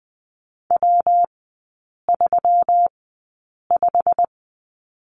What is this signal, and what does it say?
Morse code, heard as a keyed tone, about 20 wpm, spaced as for 8 wpm, 700 Hz. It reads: W35
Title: 166 Lesson 27e - Number 7 - Three Characters 20wpm